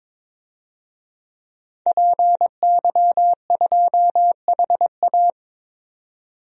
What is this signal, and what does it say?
Morse code, heard as a keyed tone, about 21 wpm, 700 Hz. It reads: PY2HA
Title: DXCC-100 Call Signs - 5 Characters - 22wpm